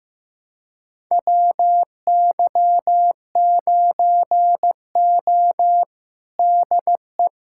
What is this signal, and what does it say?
Morse code, heard as a keyed tone, about 15 wpm, 700 Hz. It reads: WY9O DE
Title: USA Callsign DE Callsign - 4 Characters - 15wpm